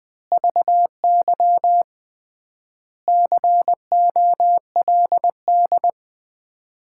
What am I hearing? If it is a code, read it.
Morse code, heard as a keyed tone, about 21 wpm, 700 Hz. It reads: VY COLD
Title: Common QSO Phrases - 3x Word-Spacing - 20wpm